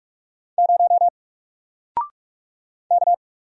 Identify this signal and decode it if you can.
Morse code, heard as a keyed tone, about 46 wpm, 700 Hz. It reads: 0 K
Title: Single Character (Letter:Number:Punctuation:Prosign) 45wpm